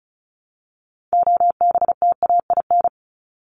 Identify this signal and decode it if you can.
Morse code, heard as a keyed tone, about 34 wpm, 700 Hz. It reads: OBTAIN
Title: ED Suffix - Top 100 Words with Base Words 35wpm